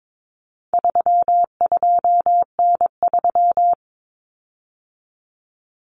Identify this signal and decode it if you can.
Morse code, heard as a keyed tone, about 22 wpm, 700 Hz. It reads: 32N3
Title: Common Cut Numbers - 4-Digit 22wpm